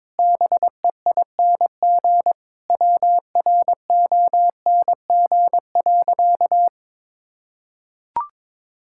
Morse code, heard as a keyed tone, about 22 wpm, 700 Hz. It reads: BEING WRONG.
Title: Sentences from Top 1000 Words - Review Part 6 - 22wpm